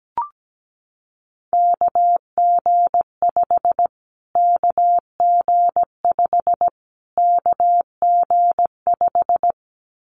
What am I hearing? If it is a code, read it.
Morse code, heard as a keyed tone, about 17 wpm, 700 Hz. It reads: KG5 KG5 KG5
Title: Callsign Prefixes for Top 25 DXCC - Repeated 3x - 17wpm